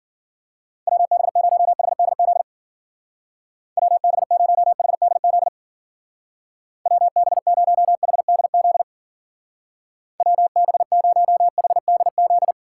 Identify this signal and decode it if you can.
Morse code, heard as a keyed tone, about 50 wpm, 700 Hz. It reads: WB0HDZ WB0HDZ WB0HDZ WB0HDZ
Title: Speed Racing - USA 2x3 Call Signs 40wpm